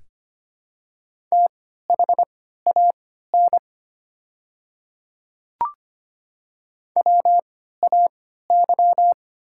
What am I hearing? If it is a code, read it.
Morse code, heard as a keyed tone, about 25 wpm, spaced as for 14 wpm, 700 Hz. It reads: THAN WAY
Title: 3x Character Spacing - Top 100 Words - 25wpm